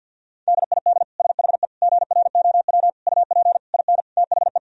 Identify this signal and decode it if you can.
Morse code, heard as a keyed tone, about 51 wpm, 700 Hz. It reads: DID SHE GROW UP IN THE
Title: Sentences from Top 1000 Words - Review Part 2 - 50wpm